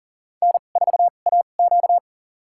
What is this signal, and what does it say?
Morse code, heard as a keyed tone, about 40 wpm, spaced as for 29 wpm, 700 Hz. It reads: N4AQ
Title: USA Call Signs - 2x Character Spacing - 40wpm